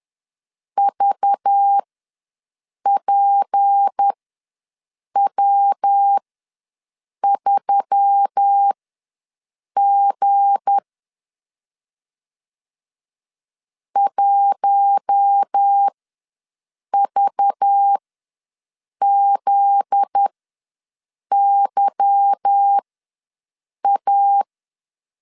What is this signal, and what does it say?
Morse code, heard as a keyed tone, about 10 wpm, spaced as for 6 wpm, 800 Hz. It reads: VPW3G 1VZYA